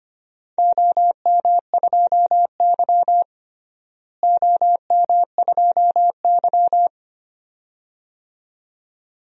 Morse code, heard as a keyed tone, about 25 wpm, 700 Hz. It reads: OM2Y OM2Y
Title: DXCC-100 Call Signs - 4 Characters - Repeated 2x - 25wpm